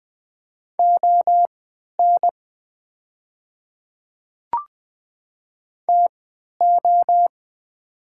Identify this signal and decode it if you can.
Morse code, heard as a keyed tone, about 20 wpm, spaced as for 11 wpm, 700 Hz. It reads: ON TO
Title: 3x Character Spacing - Top 100 Words - 20wpm